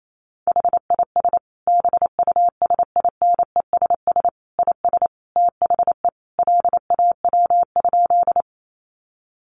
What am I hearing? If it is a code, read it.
Morse code, heard as a keyed tone, about 28 wpm, 700 Hz. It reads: HIS BUSINESS IS THE LAW?
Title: Sentences from Top 600 Words 28wpm